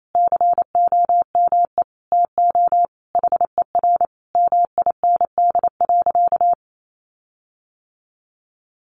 Morse code, heard as a keyed tone, about 28 wpm, 700 Hz. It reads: COME TO HER MIND.
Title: Sentences from Top 1300 Words 28wpm